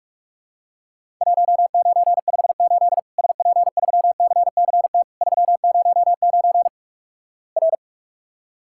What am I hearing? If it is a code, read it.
Morse code, heard as a keyed tone, about 46 wpm, 700 Hz. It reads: 1958 SP3CCT 309 R
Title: Worked All Europe DX Contest - QTC Exchange - 45wpm